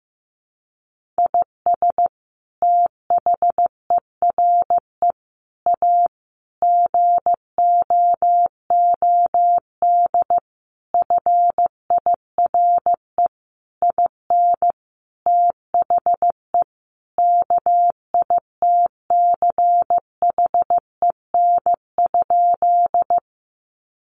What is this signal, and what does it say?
Morse code, heard as a keyed tone, about 15 wpm, 700 Hz. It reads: IS THERE A GOOD FIRE IN THE KITCHEN?
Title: Sentences from Top 1400 Words 15wpm